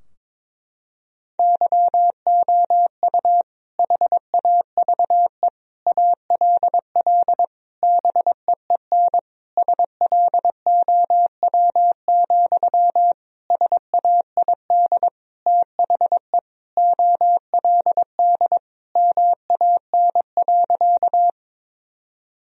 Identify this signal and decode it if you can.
Morse code, heard as a keyed tone, about 22 wpm, 700 Hz. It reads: YOU HAVE ALL BEEN SLOW, SAID THE OLD MAN.
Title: Sentences from Top 1000 Words - Review Part 2 - 22wpm